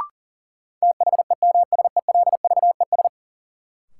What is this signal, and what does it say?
Morse code, heard as a keyed tone, about 40 wpm, 700 Hz. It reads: THEMSELVES
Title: Sentences from Top 800 Words - New Word Drill - 40wpm